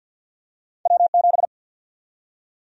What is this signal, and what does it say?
Morse code, heard as a keyed tone, about 53 wpm, 700 Hz. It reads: W7
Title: USA Call Sign Prefixes - 50wpm